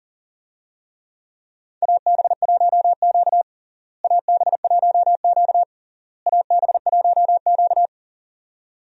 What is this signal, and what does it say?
Morse code, heard as a keyed tone, about 40 wpm, 700 Hz. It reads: AB1Q AB1Q AB1Q
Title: USA Call Signs - 4 Characters - Repeated 3x - 40wpm